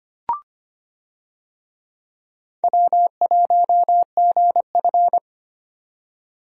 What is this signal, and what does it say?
Morse code, heard as a keyed tone, about 25 wpm, 700 Hz. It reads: W1GF
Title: USA Call Signs - 4 Characters - 25wpm